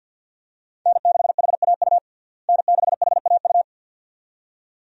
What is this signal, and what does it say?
Morse code, heard as a keyed tone, about 51 wpm, 700 Hz. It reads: N6HAU N6HAU
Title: USA Call Signs - 5 Characters - Repeated 2x - 50wpm